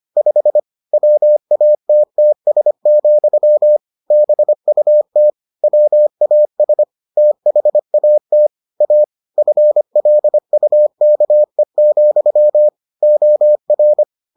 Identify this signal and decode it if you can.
Morse code, heard as a keyed tone, about 25 wpm, 600 Hz. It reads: 5 WATTS, BUT WAS THAT A FLUKE, OR